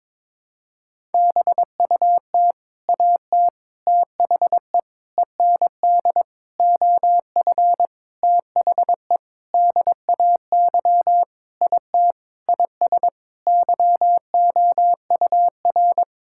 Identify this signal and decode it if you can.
Morse code, heard as a keyed tone, about 22 wpm, 700 Hz. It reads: BUT AT THE END OF THE DAY IT IS YOUR
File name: practice / 22wpm / Sentences with 8-Word Phrases - 22wpm